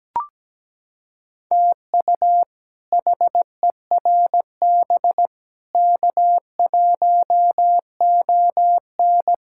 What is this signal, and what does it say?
Morse code, heard as a keyed tone, about 17 wpm, 700 Hz. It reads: TU HERB K1ON
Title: K1USN SST QSO Practice - 17wpm